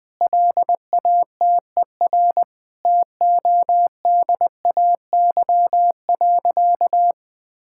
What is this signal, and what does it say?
Morse code, heard as a keyed tone, about 20 wpm, 700 Hz. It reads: LATER TODAY.
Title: Sentences from Top 1000 Words - Review Part 6 - 20wpm